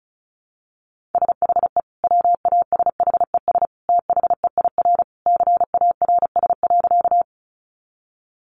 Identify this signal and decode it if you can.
Morse code, heard as a keyed tone, about 34 wpm, 700 Hz. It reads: SHE WASHES THEIR CARS.
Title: Easy Phrases 35wpm